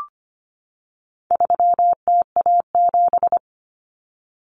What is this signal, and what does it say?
Morse code, heard as a keyed tone, about 25 wpm, 700 Hz. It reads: 3TA7
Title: Common Cut Numbers - 4-Digit 25wpm